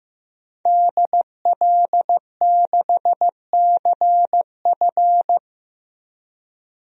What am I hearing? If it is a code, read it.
Morse code, heard as a keyed tone, about 15 wpm, 700 Hz. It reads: DL6CF
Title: DXCC-100 Call Signs - 5 Characters - Repeated 2x - 15wpm